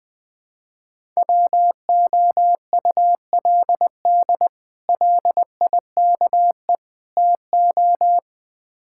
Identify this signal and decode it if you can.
Morse code, heard as a keyed tone, about 21 wpm, 700 Hz. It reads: WOULD LIKE TO
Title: Build A Phrase - 3-Word Phrases - 20wpm